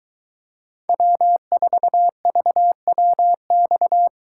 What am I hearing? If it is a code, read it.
Morse code, heard as a keyed tone, about 22 wpm, 700 Hz. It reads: W4VWX